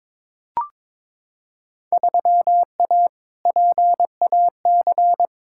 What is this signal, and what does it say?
Morse code, heard as a keyed tone, about 21 wpm, 700 Hz. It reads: 3A PAC
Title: Field Day Exchanges Encore - 22wpm